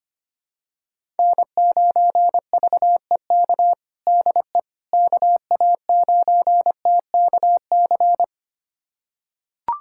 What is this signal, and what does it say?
Morse code, heard as a keyed tone, about 25 wpm, 700 Hz. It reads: N9VEK DE KA9TKC
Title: USA Callsign DE Callsign 25wpm